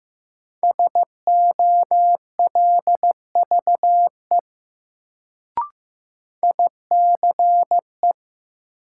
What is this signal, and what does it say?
Morse code, heard as a keyed tone, about 15 wpm, 700 Hz. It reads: SOLVE ICE
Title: Sentences from Top 1000 Words - New Word Drill - 15wpm